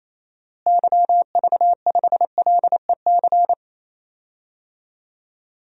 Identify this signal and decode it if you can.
Morse code, heard as a keyed tone, about 28 wpm, 700 Hz. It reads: YV5LEC
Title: DXCC-100 Call Signs - 6 Characters - Repeated 3x - 28wpm